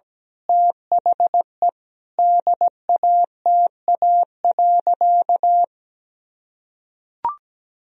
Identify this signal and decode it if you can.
Morse code, heard as a keyed tone, about 17 wpm, 700 Hz. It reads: THE DATA.
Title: Sentences from Top 1000 Words - Review Part 3 - 17wpm